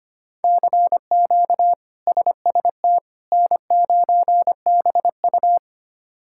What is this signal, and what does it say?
Morse code, heard as a keyed tone, about 25 wpm, 700 Hz. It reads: CQ SST N9BU
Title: K1USN SST QSO Practice - 25wpm